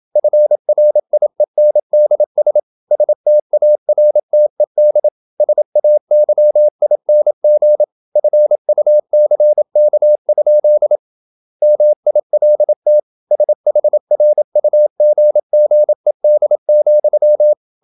Morse code, heard as a keyed tone, about 27 wpm, 600 Hz. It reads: FRIENDS STARTED SAYING FUCK? MILT SHRUGGED,
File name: Never 0027wpm